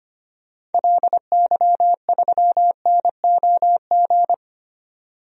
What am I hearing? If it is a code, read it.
Morse code, heard as a keyed tone, about 25 wpm, 700 Hz. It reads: LY3NOG
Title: DXCC-100 Call Signs - 6 Characters - 25wpm